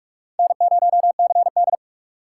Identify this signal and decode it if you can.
Morse code, heard as a keyed tone, about 46 wpm, 700 Hz. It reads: N0CD
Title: USA Call Signs - 4 Characters - 4x Character Spacing - 45wpm